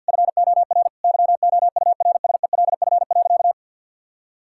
Morse code, heard as a keyed tone, about 50 wpm, 700 Hz. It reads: FOR YOURSELF.
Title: Sentences from Top 1000 Words - Review Part 6 - 50wpm